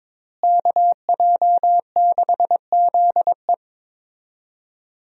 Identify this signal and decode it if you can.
Morse code, heard as a keyed tone, about 22 wpm, 700 Hz. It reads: KJ6ZE
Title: USA Call Signs - 5 Characters - Repeated 3x - 22wpm